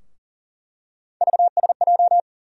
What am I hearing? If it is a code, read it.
Morse code, heard as a keyed tone, about 40 wpm, 700 Hz. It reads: VSJ
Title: USA Call Sign Postfixes - 3 Letters 40wpm